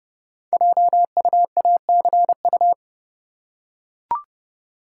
Morse code, heard as a keyed tone, about 30 wpm, 700 Hz. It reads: JUACU
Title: Code Groups - 5 Letters - 30wpm